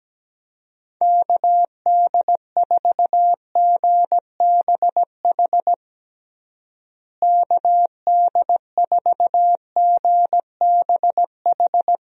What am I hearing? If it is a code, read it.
Morse code, heard as a keyed tone, about 17 wpm, 700 Hz. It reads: KD4GBH KD4GBH
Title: USA Call Signs - 6 Characters - Repeated 2x - 17wpm